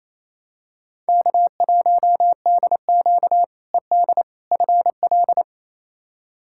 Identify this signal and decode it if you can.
Morse code, heard as a keyed tone, about 28 wpm, 700 Hz. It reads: K1DQ ED FL
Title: K1USN SST QSO Practice - 28wpm